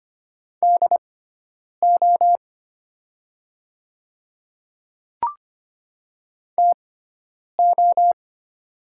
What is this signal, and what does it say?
Morse code, heard as a keyed tone, about 25 wpm, spaced as for 9 wpm, 700 Hz. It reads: DO TO
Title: 6x Character Spacing - Top 100 Words - 25wpm